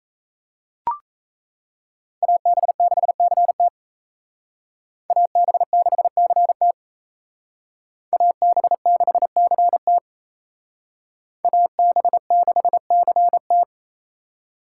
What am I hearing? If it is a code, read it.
Morse code, heard as a keyed tone, about 36 wpm, 700 Hz. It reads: AB6CT AB6CT AB6CT AB6CT